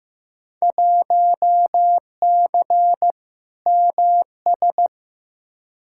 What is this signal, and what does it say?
Morse code, heard as a keyed tone, about 15 wpm, 700 Hz. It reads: 1C MS
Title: Field Day Exchanges Encore - 15wpm